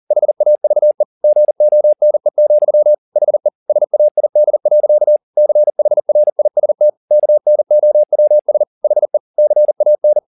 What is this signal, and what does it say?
Morse code, heard as a keyed tone, about 40 wpm, 600 Hz. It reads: HAVE GONE, HE SAID. CHRIST KNOWS HE CAN